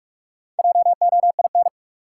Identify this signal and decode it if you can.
Morse code, heard as a keyed tone, about 44 wpm, 700 Hz. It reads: JOIN
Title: Sentences from Top 1400 Words - New Word Drill Alt - 45wpm